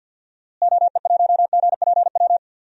Morse code, heard as a keyed tone, about 52 wpm, 700 Hz. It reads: OE1GPW
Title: DXCC-100 Call Signs - 6 Characters - Repeated 3x - 50wpm